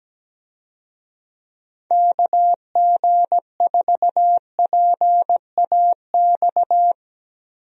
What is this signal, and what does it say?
Morse code, heard as a keyed tone, about 17 wpm, 700 Hz. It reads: KG4PAX
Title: USA Call Signs - 6 Characters - Repeated 2x - 17wpm